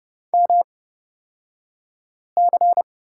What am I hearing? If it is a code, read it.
Morse code, heard as a keyed tone, about 29 wpm, spaced as for 5 wpm, 690 Hz. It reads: MC
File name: Lesson 36d - BT - Two Characters 30wpm